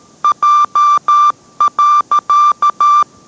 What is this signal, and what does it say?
Morse code, heard as a keyed tone, about 13 wpm, 1180 Hz. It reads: J.